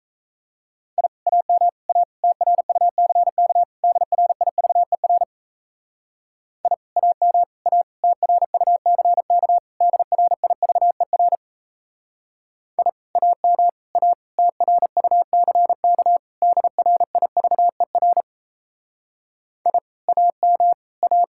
Morse code, heard as a keyed tone, about 36 wpm, 700 Hz. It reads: I AM A TRUCK DRIVER I AM A TRUCK DRIVER I AM A TRUCK DRIVER I AM A